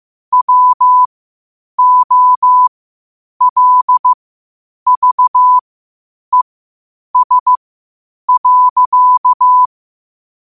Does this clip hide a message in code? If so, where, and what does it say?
Yes: Morse code from 0.3 to 9.7 s, heard as a keyed tone, about 15 wpm, spaced as for 8 wpm, 1000 Hz. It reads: WOLVES.